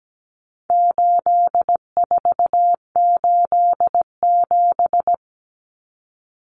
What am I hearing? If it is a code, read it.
Morse code, heard as a keyed tone, about 17 wpm, 700 Hz. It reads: 8487
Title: Common Cut Numbers - 4-Digit 17wpm